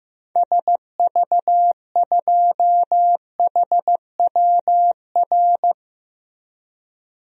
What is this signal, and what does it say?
Morse code, heard as a keyed tone, about 15 wpm, 700 Hz. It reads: SV2HWR
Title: DXCC-100 Call Signs - 6 Characters - Repeated 2x - 15wpm